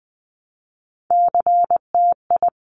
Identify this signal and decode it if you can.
Morse code, heard as a keyed tone, about 19 wpm, 700 Hz. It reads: CTI
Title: Top Three Letter Combinations in Words 20wpm